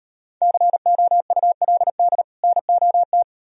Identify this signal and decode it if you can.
Morse code, heard as a keyed tone, about 38 wpm, 700 Hz. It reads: COULD NOT